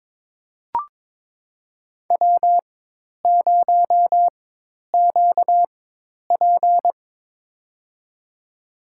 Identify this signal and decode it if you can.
Morse code, heard as a keyed tone, about 22 wpm, spaced as for 10 wpm, 700 Hz. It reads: W0QP
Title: USA Call Signs - 4 Characters - 4x Character Spacing - 22wpm